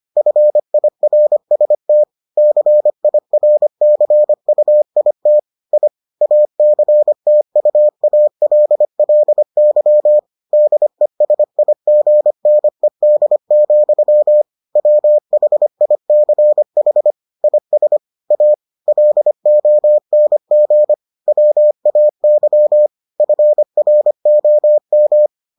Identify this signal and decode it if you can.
Morse code, heard as a keyed tone, about 25 wpm, 600 Hz. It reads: FIRST CIRCUIT I ACTUALLY DESIGNED, WHICH IS A LONG WAY FROM